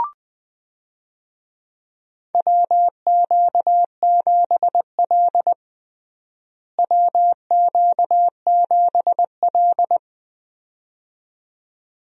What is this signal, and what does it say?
Morse code, heard as a keyed tone, about 20 wpm, 700 Hz. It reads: WQ7L WQ7L
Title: USA Call Signs - 4 Characters - Repeated 2x - 20wpm